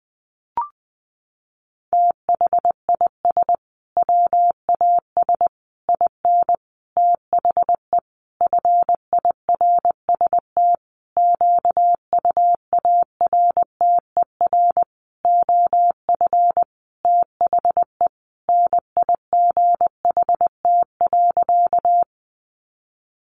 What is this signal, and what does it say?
Morse code, heard as a keyed tone, about 20 wpm, 700 Hz. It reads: THIS WAS IN THE FIRST QUARTER OF THE NIGHT.
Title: Sentences from Top 1300 Words 20wpm